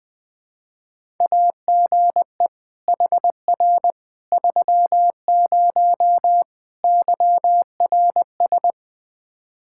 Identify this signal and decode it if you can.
Morse code, heard as a keyed tone, about 21 wpm, 700 Hz. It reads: AGE HR 30 YRS
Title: Practice QSO - Exchange 3 - More Info - 20wpm